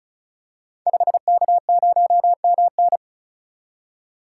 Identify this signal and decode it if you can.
Morse code, heard as a keyed tone, about 36 wpm, 700 Hz. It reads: 5K0MN
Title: Code Groups - 5 Letter:Numbers - 35wpm